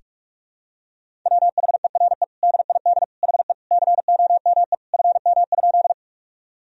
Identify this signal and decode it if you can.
Morse code, heard as a keyed tone, about 42 wpm, 700 Hz. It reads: WHERE DID HE COME FM?
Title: Sentences from Top 200 Words and Abbreviations 45wpm